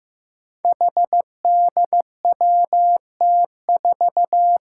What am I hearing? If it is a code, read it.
Morse code, heard as a keyed tone, about 15 wpm, 700 Hz. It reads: HDWT4
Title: Five Challenging Letter-Number Combinations 15wpm